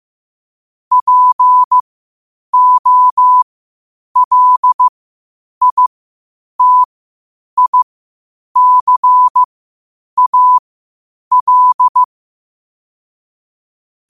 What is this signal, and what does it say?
Morse code, heard as a keyed tone, about 15 wpm, spaced as for 8 wpm, 1000 Hz. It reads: POLITICAL
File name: qotdfiles / 8wpm